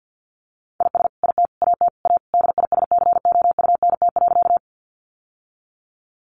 Morse code, heard as a keyed tone, about 50 wpm, 700 Hz. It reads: IS IT AT A DISCOUNT.
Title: Sentences from Top 1200 Words 50wpm